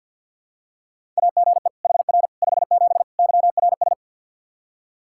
Morse code, heard as a keyed tone, about 52 wpm, 700 Hz. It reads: AGE HR 57 YRS
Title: K1USN SST QSO Practice 50wpm